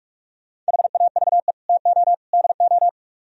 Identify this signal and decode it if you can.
Morse code, heard as a keyed tone, about 56 wpm, 700 Hz. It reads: HAVE TO DO